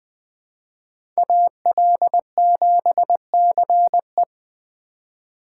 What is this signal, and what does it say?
Morse code, heard as a keyed tone, about 20 wpm, 700 Hz. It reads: AL7CE
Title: USA Call Signs - 5 Characters - 3x Character Spacing - 20wpm